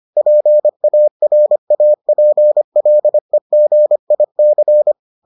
Morse code, heard as a keyed tone, about 25 wpm, 600 Hz. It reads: PARAPLEGIC